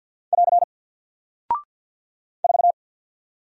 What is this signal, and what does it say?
Morse code, heard as a keyed tone, about 48 wpm, 700 Hz. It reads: <AR> 4
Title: Single Character (Letter:Number:Punctuation:Prosign) 50wpm